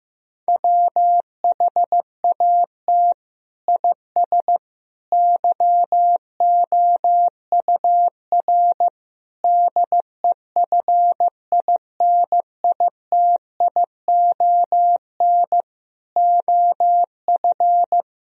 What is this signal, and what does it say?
Morse code, heard as a keyed tone, about 15 wpm, 700 Hz. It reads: WHAT IS YOUR DEFINITION OF
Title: Sentences from Top 1000 Words - Review Part 4 - 15wpm